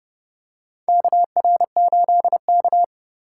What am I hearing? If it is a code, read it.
Morse code, heard as a keyed tone, about 32 wpm, 700 Hz. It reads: KR8K
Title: USA Call Signs - 4 Characters - 4x Character Spacing - 30wpm